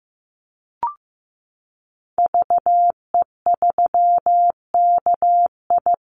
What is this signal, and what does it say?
Morse code, heard as a keyed tone, about 15 wpm, 700 Hz. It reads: VE3KI
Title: Call Signs Repeated 2x - 15wpm